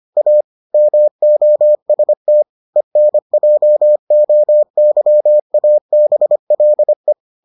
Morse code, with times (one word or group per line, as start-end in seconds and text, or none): 0.2-0.4 A
0.7-2.4 MOST
2.8-7.1 ENJOYABLE